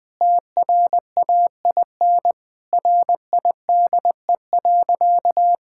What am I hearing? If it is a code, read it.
Morse code, heard as a keyed tone, about 21 wpm, 700 Hz. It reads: TRAIN RIDE.
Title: Difficult Sentences with Hints - 20wpm